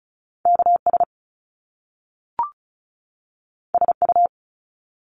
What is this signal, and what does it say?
Morse code, heard as a keyed tone, about 36 wpm, 700 Hz. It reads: KS SU
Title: Top Two Letter Combinations in Words 35wpm